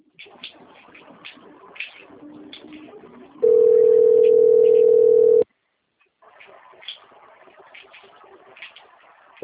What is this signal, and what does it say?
Ringback tone (North America), 440 + 480 Hz.
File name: out- - - - - .24854